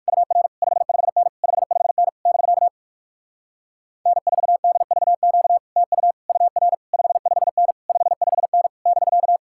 Morse code, heard as a keyed tone, about 56 wpm, 700 Hz. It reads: UR 55N 55N <BK> N4DVQ TU UR 55N 55N <BK>